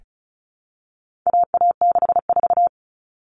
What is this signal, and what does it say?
Morse code, heard as a keyed tone, about 34 wpm, 700 Hz. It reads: AA64
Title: All Cut Numbers - 4-Digit 35wpm